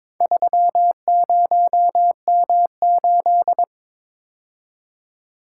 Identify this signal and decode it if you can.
Morse code, heard as a keyed tone, about 22 wpm, 700 Hz. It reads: 30M8
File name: practice / 22wpm / Four Letter-Number Combinations - 22wpm